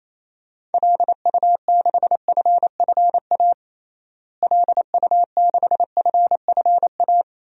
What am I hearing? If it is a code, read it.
Morse code, heard as a keyed tone, about 28 wpm, 700 Hz. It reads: LU6FFA LU6FFA
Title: DXCC-100 Call Signs - 6 Characters - Repeated 3x - 28wpm